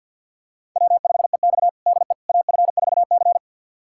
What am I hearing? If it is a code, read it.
Morse code, heard as a keyed tone, about 51 wpm, 700 Hz. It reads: W5EX DE AF4C
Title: USA Callsign DE Callsign - 4 Characters - 50wpm